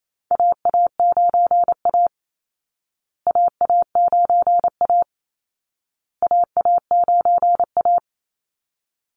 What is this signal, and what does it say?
Morse code, heard as a keyed tone, about 28 wpm, 700 Hz. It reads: AA9A AA9A AA9A
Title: Call Signs Repeated 2x - 28wpm